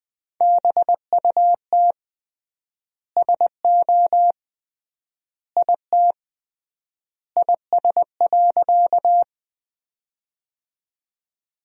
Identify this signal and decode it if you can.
Morse code, heard as a keyed tone, about 20 wpm, 700 Hz. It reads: BUT SO IT IS.
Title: 3x Word-Spacing - Sentences from Top 100 Words - 20wpm